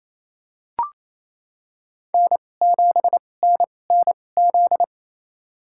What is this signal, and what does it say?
Morse code, heard as a keyed tone, about 28 wpm, spaced as for 20 wpm, 700 Hz. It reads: N7NNZ
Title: USA Call Signs - 2x Character Spacing - 28wpm